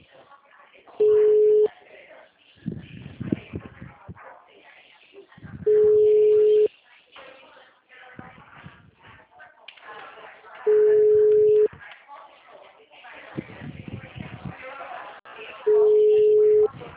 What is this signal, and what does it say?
Ringback tone (Europe, 425 Hz), with speech.